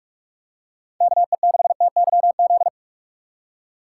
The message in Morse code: KE6TYZ

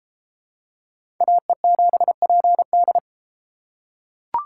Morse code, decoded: AE7PD

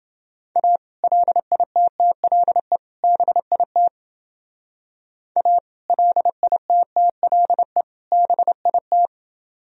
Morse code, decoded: A LITTLE BIT A LITTLE BIT

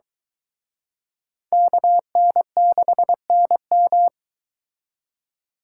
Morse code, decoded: KN6NM

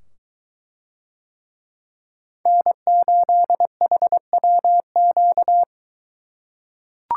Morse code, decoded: N8HWQ

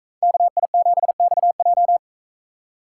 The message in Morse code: KI7XJ

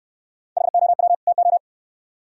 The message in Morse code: HOW TO

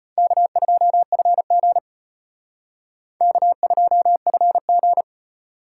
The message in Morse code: K2FG K2FG

